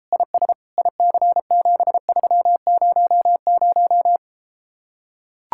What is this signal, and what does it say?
Morse code, heard as a keyed tone, about 33 wpm, 700 Hz. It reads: IS IC7300